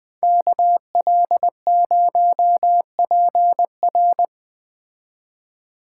Morse code, heard as a keyed tone, about 20 wpm, 700 Hz. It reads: KL0PR